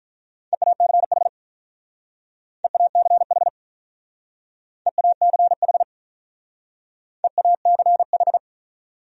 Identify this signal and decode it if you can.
Morse code, heard as a keyed tone, about 44 wpm, 700 Hz. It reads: EACH EACH EACH EACH